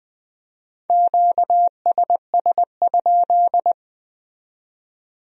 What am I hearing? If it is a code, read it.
Morse code, heard as a keyed tone, about 20 wpm, 700 Hz. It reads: QSS?
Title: Speed-Racing Q-Codes - 17wpm